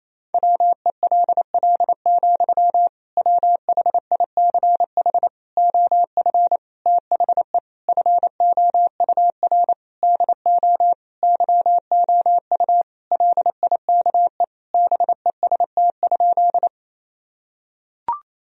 Morse code, heard as a keyed tone, about 27 wpm, 700 Hz. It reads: WELL, WHICH OF THE FOUR DO YOU LIKE BEST?
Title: Sentences from Top 1000 Words - Review Part 1 - 28wpm